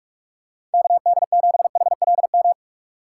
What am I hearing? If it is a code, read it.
Morse code, heard as a keyed tone, about 46 wpm, 700 Hz. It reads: KD7HLM
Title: USA Call Signs - 6 Characters - 2x Character Spacing - 45wpm